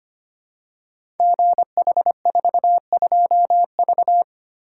Morse code, heard as a keyed tone, about 25 wpm, 700 Hz. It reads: GH42V
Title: Code Groups - 5 Letter:Numbers - 25wpm